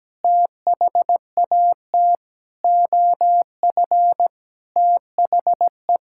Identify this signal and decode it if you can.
Morse code, heard as a keyed tone, about 17 wpm, 700 Hz. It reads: THAT OF THE
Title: Sentences from Top 1600 Words 17wpm